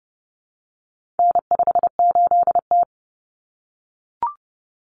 Morse code, heard as a keyed tone, about 30 wpm, 700 Hz. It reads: N58T
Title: Common Cut Numbers - 4-Digit 30wpm